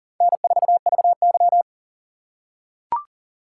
Morse code, heard as a keyed tone, about 40 wpm, 700 Hz. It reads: N4VY
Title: USA Call Signs - 4 Characters - 6x Character Spacing - 40wpm